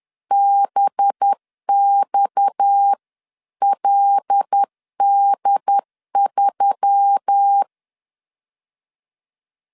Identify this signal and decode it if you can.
Morse code, heard as a keyed tone, about 10 wpm, 800 Hz. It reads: BXLD3
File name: CW / DET / 1 9 1500C S A 10wpm 4511